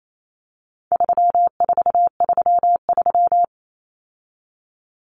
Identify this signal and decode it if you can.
Morse code, heard as a keyed tone, about 28 wpm, 700 Hz. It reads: 3433
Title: Numbers - 4-Digit 28wpm